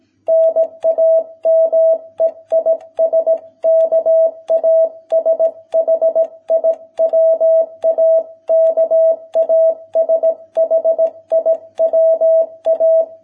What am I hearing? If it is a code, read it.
Morse code, heard as a keyed tone, about 18 wpm, 650 Hz. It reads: NAME IS KASHIWA KASHIWA